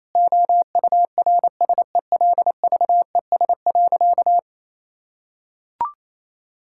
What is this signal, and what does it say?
Morse code, heard as a keyed tone, about 28 wpm, 700 Hz. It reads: OURSELVES.
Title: Sentences from Top 1600 Words 28wpm